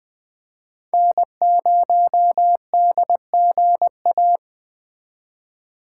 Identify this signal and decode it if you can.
Morse code, heard as a keyed tone, about 20 wpm, 700 Hz. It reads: N0DGA
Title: Speed Racing - USA Call Signs 17wpm